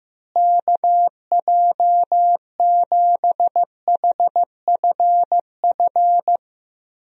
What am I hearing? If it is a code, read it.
Morse code, heard as a keyed tone, about 15 wpm, 700 Hz. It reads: KJ7HFF
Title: USA Call Signs - 6 Characters - Repeated 3x - 15wpm